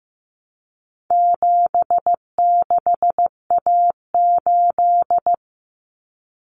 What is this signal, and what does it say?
Morse code, heard as a keyed tone, about 15 wpm, 700 Hz. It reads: 76A8
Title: Common Cut Numbers - 4-Digit 15wpm